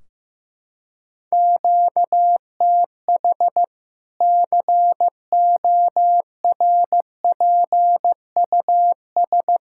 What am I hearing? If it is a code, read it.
Morse code, heard as a keyed tone, about 15 wpm, 700 Hz. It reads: QTH CORPUS
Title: K1USN SST QSO Practice 15wpm